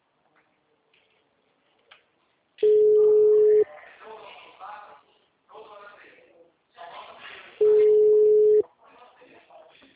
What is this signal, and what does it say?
Ringback tone (Europe, 425 Hz), with speech.